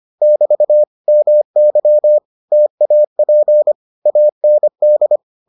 Morse code, heard as a keyed tone, about 25 wpm, 600 Hz. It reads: <BT> MY TAP AND